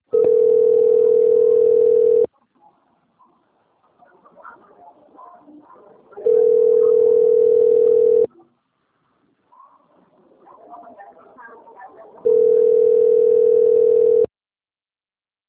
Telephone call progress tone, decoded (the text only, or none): ringback tone